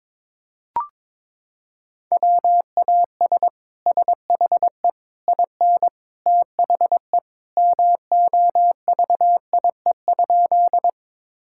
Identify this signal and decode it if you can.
Morse code, heard as a keyed tone, about 22 wpm, 700 Hz. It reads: WAS SHE IN THE MOVIE?
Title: Sentences from Top 1000 Words - Review Part 5 - 22wpm